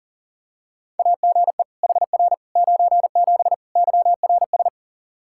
Morse code, heard as a keyed tone, about 37 wpm, 700 Hz. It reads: AGE HR 97 YRS
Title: Practice QSO - Exchange 3 - More Info - 40wpm